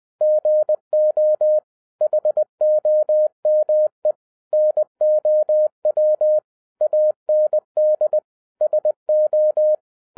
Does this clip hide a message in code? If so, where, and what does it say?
Yes: Morse code from 0.2 to 9.7 s, heard as a keyed tone, about 20 wpm, 610 Hz. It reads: GO HOME NOW AND SO